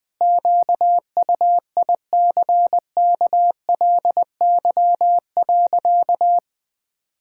Morse code, heard as a keyed tone, about 20 wpm, 700 Hz. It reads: QUICKLY.
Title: Sentences with 6-Word Phrases - 20wpm